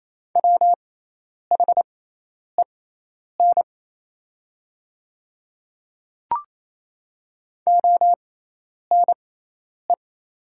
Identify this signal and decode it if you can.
Morse code, heard as a keyed tone, about 28 wpm, spaced as for 10 wpm, 690 Hz. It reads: WHEN ONE
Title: 6x Character Spacing - Top 200 Words - 28wpm